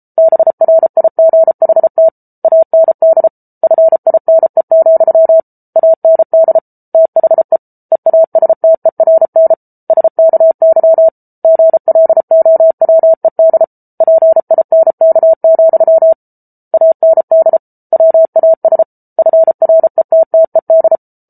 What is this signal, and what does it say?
Morse code, heard as a keyed tone, about 34 wpm, 650 Hz. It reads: BRIGHT AND FINE, AND THE EASTERN SKY GLOWED PINK, AND WAS FRETTED